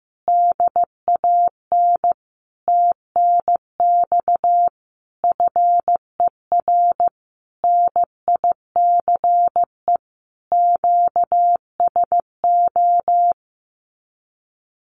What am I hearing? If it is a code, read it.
Morse code, heard as a keyed tone, about 15 wpm, 700 Hz. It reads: DAN TNX FER NICE QSO
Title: Common QSO Phrases Encore 15wpm